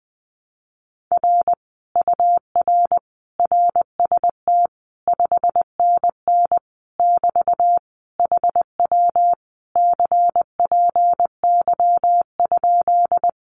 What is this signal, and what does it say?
Morse code, heard as a keyed tone, about 20 wpm, 700 Hz. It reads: R UR RST 5NN <BT> HW CPY?
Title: Common QSO Phrases Encore 20wpm